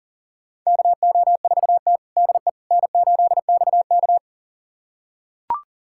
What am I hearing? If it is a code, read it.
Morse code, heard as a keyed tone, about 40 wpm, 700 Hz. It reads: KO4T DE N8XK
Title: USA Callsign DE Callsign - 4 Characters - 40wpm